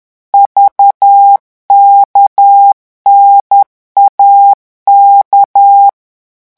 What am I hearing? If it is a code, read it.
Morse code, heard as a keyed tone, about 10 wpm, 800 Hz. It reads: VKNAK
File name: morse mp3 / revision5-vitesse 10 mots